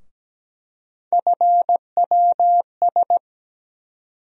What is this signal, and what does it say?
Morse code, heard as a keyed tone, about 17 wpm, 700 Hz. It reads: FWS
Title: USA Call Sign Postfixes - 3 Letters 17wpm